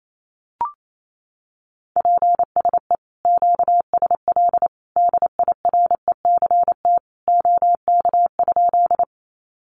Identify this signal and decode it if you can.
Morse code, heard as a keyed tone, about 28 wpm, 700 Hz. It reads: PSE QSL DIRECT OK?
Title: Common QSO Phrases 28wpm